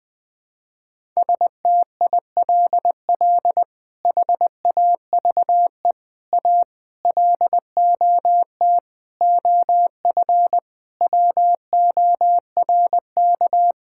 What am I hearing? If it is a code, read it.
Morse code, heard as a keyed tone, about 20 wpm, 700 Hz. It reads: STILL HAVE A LOT OF WORK
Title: Sets of 8 Words - 20wpm